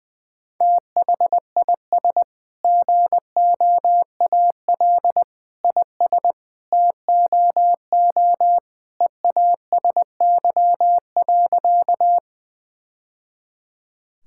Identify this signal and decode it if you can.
Morse code, heard as a keyed tone, about 20 wpm, 700 Hz. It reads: THIS GOAL IS TOO EASY.
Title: Sentences from Top 1000 Words - Review Part 5 - 20wpm